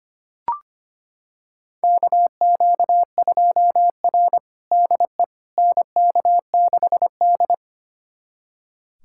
Morse code, heard as a keyed tone, about 25 wpm, 700 Hz. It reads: KQ2R DE NK6D
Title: USA Callsign DE Callsign - 4 Characters - 25wpm